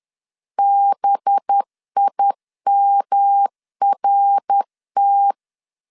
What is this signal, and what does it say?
Morse code, heard as a keyed tone, about 10 wpm, 800 Hz. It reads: BIMRT
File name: LEZIONI / 6 PKX0 / 6 PKX 10wpm 2426